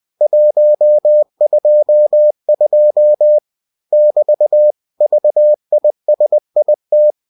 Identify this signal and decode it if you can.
Morse code, heard as a keyed tone, about 20 wpm, 600 Hz. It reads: 122 <BT> VISIT